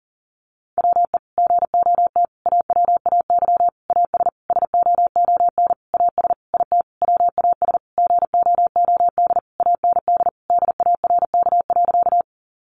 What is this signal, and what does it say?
Morse code, heard as a keyed tone, about 40 wpm, 700 Hz. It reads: WE GOT AWAY AS SOON AS IT WAS GOOD AND DARK.
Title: Sentences from Top 600 Words 40wpm